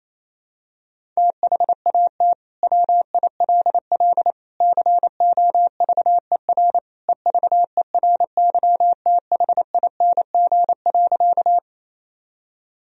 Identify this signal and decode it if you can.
Morse code, heard as a keyed tone, about 28 wpm, 700 Hz. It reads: THAT WILL COVER EVERYTHING.
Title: Sentences from Top 1000 Words - Review Part 1 - 28wpm